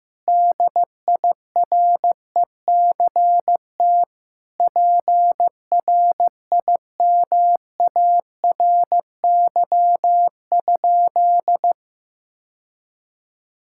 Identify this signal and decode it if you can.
Morse code, heard as a keyed tone, about 16 wpm, 700 Hz. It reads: DIRECT PRIMARY?
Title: Sentences from Top 1000 Words - Review Part 5 - 15wpm